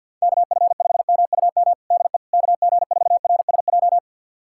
Morse code, heard as a keyed tone, about 52 wpm, 700 Hz. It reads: KF5MUM DE KG4RSJ
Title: USA Callsign DE Callsign - 6 Characters - 50wpm